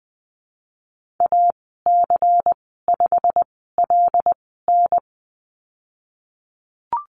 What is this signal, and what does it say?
Morse code, heard as a keyed tone, about 20 wpm, spaced as for 15 wpm, 700 Hz. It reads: AC5LN
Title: 167 Lesson 27g - Number 7 - Call Signs 20wpm